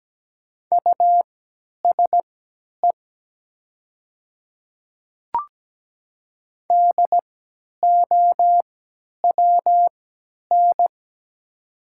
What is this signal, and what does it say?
Morse code, heard as a keyed tone, about 17 wpm, spaced as for 10 wpm, 700 Hz. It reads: USE DOWN
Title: 3x Character Spacing - Top 200 Words - 17wpm